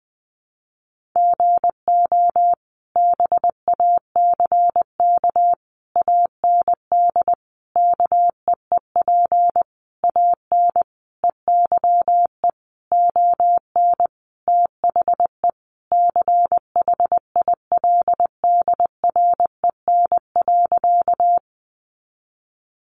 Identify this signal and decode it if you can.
Morse code, heard as a keyed tone, about 20 wpm, 700 Hz. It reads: GO BACK AND KEEP AN EYE ON THE CHILDREN.
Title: Sentences from Top 300 Words 20wpm